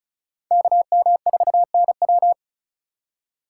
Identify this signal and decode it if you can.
Morse code, heard as a keyed tone, about 36 wpm, 700 Hz. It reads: KM4NW
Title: USA Call Signs - 5 Characters - 3x Character Spacing - 35wpm